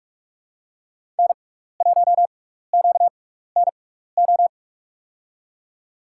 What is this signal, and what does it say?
Morse code, heard as a keyed tone, about 43 wpm, spaced as for 15 wpm, 700 Hz. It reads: N1QNO
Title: USA Call Signs - 5 Characters - 6x Character Spacing - 45wpm